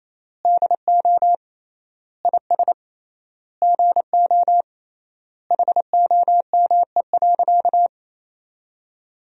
Morse code, heard as a keyed tone, about 28 wpm, 700 Hz. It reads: DO IS GO HOME.
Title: 3x Word-Spacing - Sentences from Top 200 Words - 28wpm